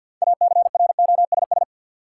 Morse code, heard as a keyed tone, about 51 wpm, 700 Hz. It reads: ACROSS
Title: Sets of 2 Words - New Word Drill - 50wpm